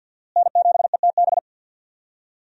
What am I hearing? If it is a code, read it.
Morse code, heard as a keyed tone, about 50 wpm, 700 Hz. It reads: N7ETB
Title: USA Call Signs - 5 Characters - Repeated 2x - 50wpm